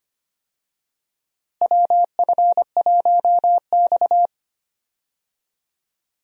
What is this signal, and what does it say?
Morse code, heard as a keyed tone, about 25 wpm, 700 Hz. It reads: WF1X
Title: USA Call Signs - 4 Characters - Repeated 2x - 25wpm